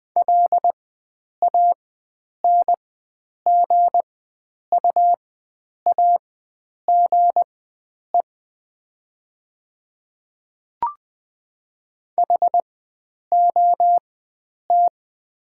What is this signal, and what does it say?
Morse code, heard as a keyed tone, about 20 wpm, spaced as for 9 wpm, 700 Hz. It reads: LANGUAGE HOT
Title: 4x Character Spacing - Top 500 Words - 20wpm